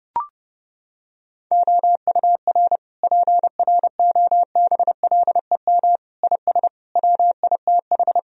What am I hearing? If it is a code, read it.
Morse code, heard as a keyed tone, about 32 wpm, 700 Hz. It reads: OUR PROBLEM IS WITH
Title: Sentences from Top 1000 Words - Review Part 2 - 30wpm